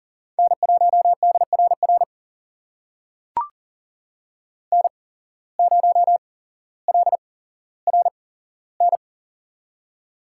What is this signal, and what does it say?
Morse code, heard as a keyed tone, about 40 wpm, spaced as for 17 wpm, 700 Hz. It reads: N1DRR N0LRN
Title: USA Call Signs - 5 Characters - 8x Character Spacing - 40wpm